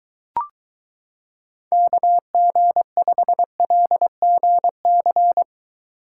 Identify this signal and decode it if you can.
Morse code, heard as a keyed tone, about 23 wpm, 700 Hz. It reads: KG5LGC